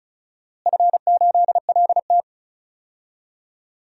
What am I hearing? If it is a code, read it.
Morse code, heard as a keyed tone, about 36 wpm, 700 Hz. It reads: F8LT
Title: DXCC-100 Call Signs - 4 Characters - Repeated 2x - 35wpm